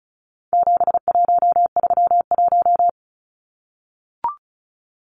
Morse code, heard as a keyed tone, about 34 wpm, 700 Hz. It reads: 7131
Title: Numbers - 4-Digit 35wpm